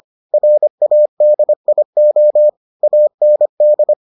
Morse code, heard as a keyed tone, about 25 wpm, 600 Hz. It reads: RADIO AND